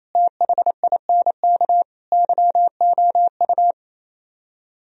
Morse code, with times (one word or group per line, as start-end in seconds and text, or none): 0.1-1.8 THINK
2.1-3.7 YOU